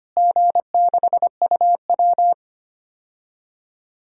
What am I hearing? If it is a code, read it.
Morse code, heard as a keyed tone, about 25 wpm, 700 Hz. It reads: G6UW
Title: DXCC-100 Call Signs - 4 Characters - Repeated 2x - 25wpm